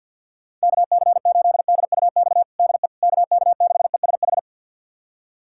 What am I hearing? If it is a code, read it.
Morse code, heard as a keyed tone, about 50 wpm, 700 Hz. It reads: KC8DUX DE KK6ESH
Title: USA Callsign DE Callsign 50wpm